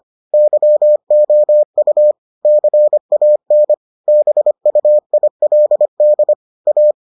Morse code, heard as a keyed tone, about 25 wpm, 600 Hz. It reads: YOU CAN BUILD A